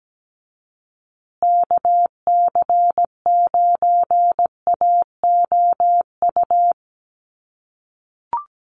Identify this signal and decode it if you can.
Morse code, heard as a keyed tone, about 17 wpm, 700 Hz. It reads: KC9AOU
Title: USA Call Signs Repeated 3x 17wpm